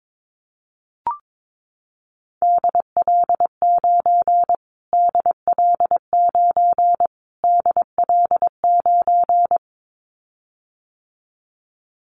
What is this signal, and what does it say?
Morse code, heard as a keyed tone, about 22 wpm, 700 Hz. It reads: DL9 DL9 DL9
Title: Callsign Prefixes for Top 25 DXCC - Repeated 3x - 22wpm